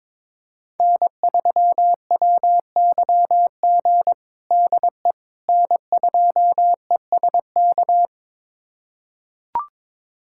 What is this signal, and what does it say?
Morse code, heard as a keyed tone, about 21 wpm, 700 Hz. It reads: N3WYG DE N2ESK
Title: USA Callsign DE Callsign - 5 Characters - 22wpm